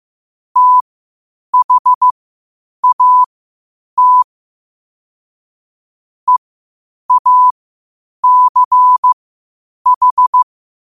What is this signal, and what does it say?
Morse code, heard as a keyed tone, about 15 wpm, spaced as for 8 wpm, 1000 Hz. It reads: THAT EACH
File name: qotdfiles / 8wpm